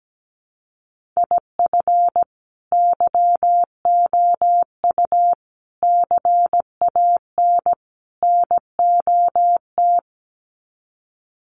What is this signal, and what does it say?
Morse code, heard as a keyed tone, about 17 wpm, 700 Hz. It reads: IF YOU CAN NOT
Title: Sets of 4 Words 17wpm